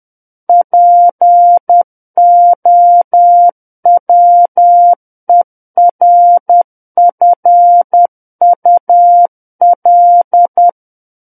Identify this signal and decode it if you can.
Morse code, heard as a keyed tone, about 10 wpm, 700 Hz. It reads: POWERFUL
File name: Book One / 01-Chapter 1 - 10 WPM